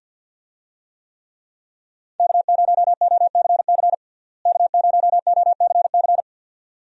Morse code, heard as a keyed tone, about 51 wpm, 700 Hz. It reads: K0OCC K0OCC
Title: USA Call Signs - 5 Characters - Repeated 2x - 50wpm